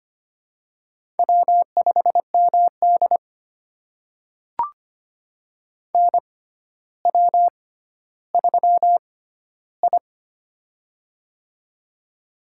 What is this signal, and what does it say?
Morse code, heard as a keyed tone, about 25 wpm, spaced as for 13 wpm, 700 Hz. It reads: W5MD NW3I
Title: USA Call Signs - 4 Characters - 6x Character Spacing - 25wpm